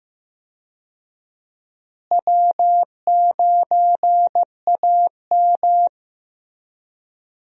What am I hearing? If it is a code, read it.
Morse code, heard as a keyed tone, about 15 wpm, 700 Hz. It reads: W9AM
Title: USA Call Signs - 4 Characters - Repeated 3x - 15wpm